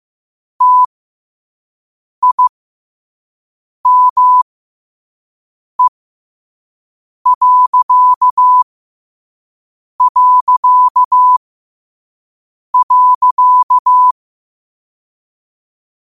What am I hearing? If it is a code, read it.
Morse code, heard as a keyed tone, about 15 wpm, spaced as for 5 wpm, 1000 Hz. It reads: TIME...
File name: qotdfiles / 5wpm